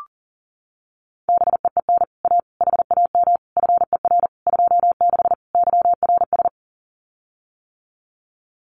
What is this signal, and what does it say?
Morse code, heard as a keyed tone, about 42 wpm, 700 Hz. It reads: BEEN A HAM FER 26 YRS
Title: Common QSO Phrases Encore 40wpm